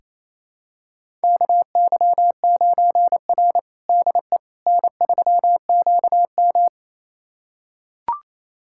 Morse code, heard as a keyed tone, about 28 wpm, 700 Hz. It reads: KY9R DE N3QM
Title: USA Callsign DE Callsign - 4 Characters - 28wpm